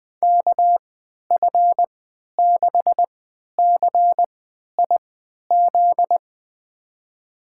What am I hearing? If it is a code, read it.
Morse code, heard as a keyed tone, about 20 wpm, spaced as for 11 wpm, 700 Hz. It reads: KF6CIZ
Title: USA Call Signs - 6 Characters - 3x Character Spacing - 20wpm